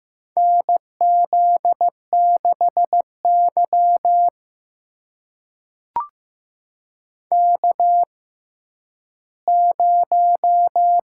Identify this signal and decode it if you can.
Morse code, heard as a keyed tone, about 15 wpm, spaced as for 10 wpm, 700 Hz. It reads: NZ6Y K0
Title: USA Call Signs - 6x Character Spacing - 15wpm